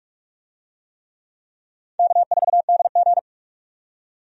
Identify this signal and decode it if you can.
Morse code, heard as a keyed tone, about 41 wpm, 700 Hz. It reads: K4DG
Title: USA Call Signs - 4 Characters - 45wpm